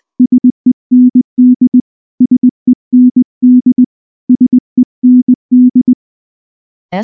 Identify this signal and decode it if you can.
Morse code, heard as a keyed tone, about 19 wpm, 250 Hz. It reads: SEND SEND SEND